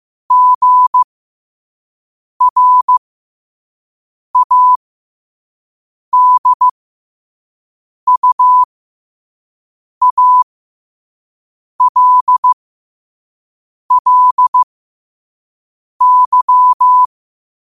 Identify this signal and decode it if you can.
Morse code, heard as a keyed tone, about 15 wpm, spaced as for 5 wpm, 1000 Hz. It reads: GRADUALLY
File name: qotdfiles / 5wpm